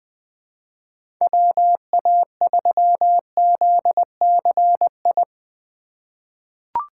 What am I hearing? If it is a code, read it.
Morse code, heard as a keyed tone, about 21 wpm, 700 Hz. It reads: WA3ZCI
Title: USA Call Signs - 4x Character Spacing - 20wpm